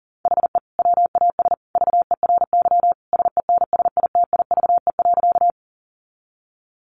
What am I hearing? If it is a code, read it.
Morse code, heard as a keyed tone, about 40 wpm, 700 Hz. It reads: HE WAS VERY SENSITIVE.
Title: Sentences from Top 1400 Words 40wpm